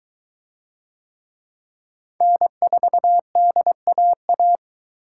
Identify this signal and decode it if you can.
Morse code, heard as a keyed tone, about 22 wpm, 700 Hz. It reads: N4DAA